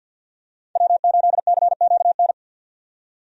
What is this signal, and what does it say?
Morse code, heard as a keyed tone, about 50 wpm, 700 Hz. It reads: W8CQN
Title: USA Call Signs - 5 Characters - 50wpm